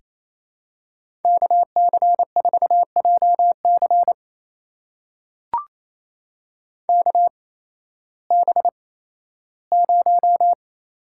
Morse code, heard as a keyed tone, about 27 wpm, spaced as for 15 wpm, 700 Hz. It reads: KC4JC KB0
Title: USA Call Signs - 5 Characters - 8x Character Spacing - 28wpm